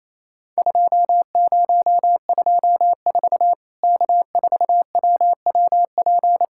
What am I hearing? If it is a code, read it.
Morse code, heard as a keyed tone, about 28 wpm, 700 Hz. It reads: 2024 K4WWP
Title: Worked All Europe DX Contest - QTC Exchange - 28wpm